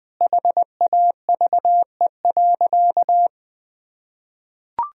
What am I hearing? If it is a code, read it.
Morse code, heard as a keyed tone, about 20 wpm, 700 Hz. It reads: HAVE.
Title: Sentences from Top 1000 Words - Review Part 4 - 20wpm